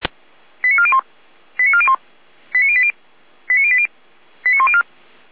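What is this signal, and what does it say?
Five-tone selective call (ZVEI), heard as quick runs of five tones, tones of about 70 ms. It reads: 89401 89401 89080 89080 89104
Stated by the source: ZVEI five-tone selective call 89092 received by SDR.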